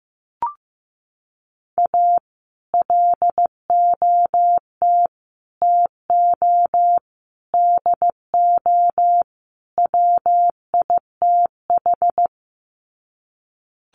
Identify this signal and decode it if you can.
Morse code, heard as a keyed tone, about 15 wpm, 700 Hz. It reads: A LOT TO DO WITH
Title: Sets of 5 Words 15wpm